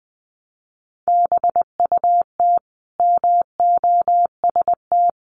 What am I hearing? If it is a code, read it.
Morse code, heard as a keyed tone, about 20 wpm, 700 Hz. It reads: BUT MOST
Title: Sentences from Top 1500 Words 20wpm